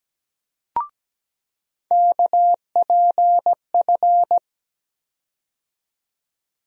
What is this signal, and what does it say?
Morse code, heard as a keyed tone, about 17 wpm, 700 Hz. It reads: KPF
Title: USA Call Sign Postfixes - 3 Letters 17wpm